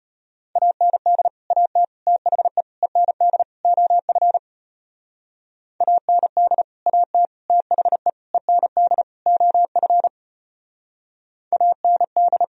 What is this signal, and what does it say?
Morse code, heard as a keyed tone, about 32 wpm, 700 Hz. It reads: AND AT THE END OF AND AT THE END OF AND